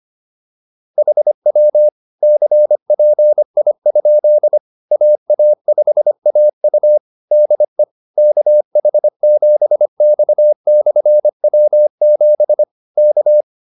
Morse code, heard as a keyed tone, about 25 wpm, 600 Hz. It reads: HW CPI? AA5AU DE KH7X/W7 K